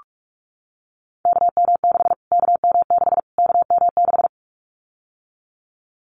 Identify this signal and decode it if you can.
Morse code, heard as a keyed tone, about 44 wpm, 700 Hz. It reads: KM6 KM6 KM6
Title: Callsign Prefixes for Top 25 DXCC - Repeated 3x - 45wpm